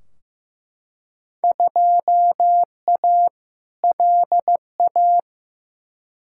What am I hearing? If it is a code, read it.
Morse code, heard as a keyed tone, about 15 wpm, 700 Hz. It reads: 2A LA
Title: Field Day Exchanges - 15wpm